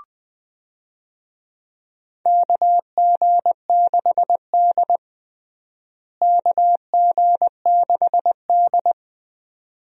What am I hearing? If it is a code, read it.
Morse code, heard as a keyed tone, about 20 wpm, 700 Hz. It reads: KG6D KG6D
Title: USA Call Signs - 4 Characters - Repeated 2x - 20wpm